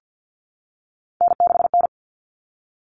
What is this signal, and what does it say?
Morse code, heard as a keyed tone, about 48 wpm, 700 Hz. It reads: N6N
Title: All Cut Numbers - 3-Digit 50wpm